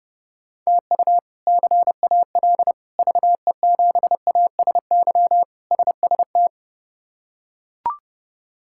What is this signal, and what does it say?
Morse code, heard as a keyed tone, about 32 wpm, 700 Hz. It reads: TU CAL VE7ASY SST
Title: K1USN SST QSO Practice - 30wpm